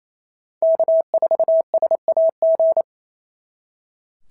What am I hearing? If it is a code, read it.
Morse code, heard as a keyed tone, about 25 wpm, 650 Hz. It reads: K4SAG
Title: SOTA QSO Practice 28wpm